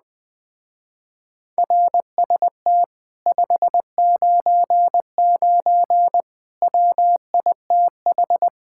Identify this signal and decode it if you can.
Morse code, heard as a keyed tone, about 20 wpm, 700 Hz. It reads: RST 599 WITH
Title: Speed-Racing Common QSO Phrases - 17wpm